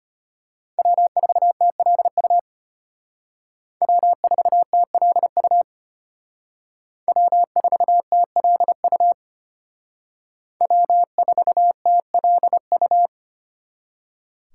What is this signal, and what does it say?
Morse code, heard as a keyed tone, about 32 wpm, 700 Hz. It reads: W4TLU W4TLU W4TLU W4TLU